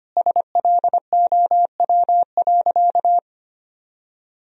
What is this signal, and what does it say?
Morse code, heard as a keyed tone, about 25 wpm, 700 Hz. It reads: SLOW.
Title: Sentences from Top 1000 Words - Review Part 2 - 25wpm